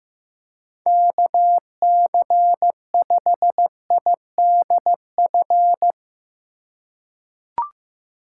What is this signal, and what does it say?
Morse code, heard as a keyed tone, about 15 wpm, 700 Hz. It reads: KC5IDF
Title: USA Call Signs - 3x Character Spacing - 15wpm